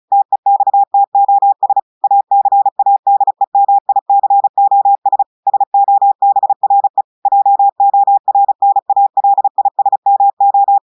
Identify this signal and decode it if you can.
Morse code, heard as a keyed tone, about 36 wpm, 800 Hz. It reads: TEXTOS ACADEMICOS SOBRE JORNALISMO
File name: audios / portugais / portugais texte en clair 1 vitesse 35 mots minutes